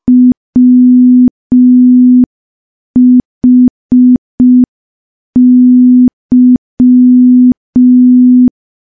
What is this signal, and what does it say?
Morse code, heard as a keyed tone, about 5 wpm, 250 Hz. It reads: WHY
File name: morse / morse-training-05WPM